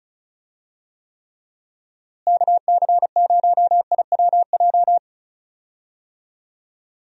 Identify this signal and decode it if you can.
Morse code, heard as a keyed tone, about 34 wpm, 700 Hz. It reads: KC0IWJ
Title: USA Call Signs - 6 Characters - Repeated 3x - 35wpm